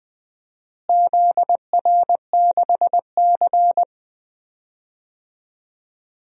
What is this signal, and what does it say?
Morse code, heard as a keyed tone, about 20 wpm, 700 Hz. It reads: ZR6C
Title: DXCC-100 Call Signs - 4 Characters - Repeated 3x - 20wpm